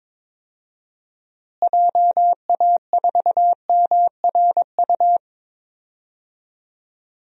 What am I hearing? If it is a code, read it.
Morse code, heard as a keyed tone, about 22 wpm, 700 Hz. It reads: JA4MRU
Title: DXCC-100 Call Signs - 6 Characters - Repeated 2x - 22wpm